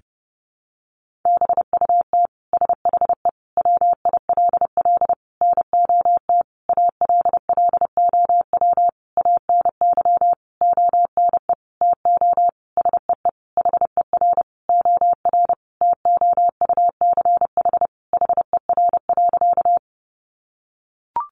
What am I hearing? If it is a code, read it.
Morse code, heard as a keyed tone, about 30 wpm, 700 Hz. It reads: BUT SHE WILL NOT ALLOW ANY ONE TO SEE HER OR TOUCH HER.
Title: Sentences from Top 1200 Words 30wpm